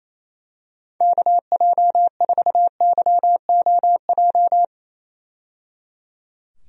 Morse code, heard as a keyed tone, about 28 wpm, 700 Hz. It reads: KJ4YOJ
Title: USA Call Signs - 6 Characters - 28wpm